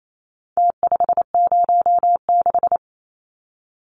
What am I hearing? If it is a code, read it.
Morse code, heard as a keyed tone, about 28 wpm, 700 Hz. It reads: T506
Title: Numbers - Contest Style - 4-Digit 28wpm